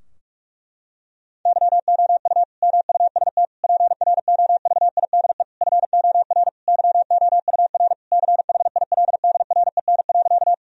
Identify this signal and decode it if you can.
Morse code, heard as a keyed tone, about 46 wpm, 700 Hz. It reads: YOU MUST PROVIDE FOR YOUR CHILDREN.
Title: Sentences from Top 1000 Words - Review Part 3 - 45wpm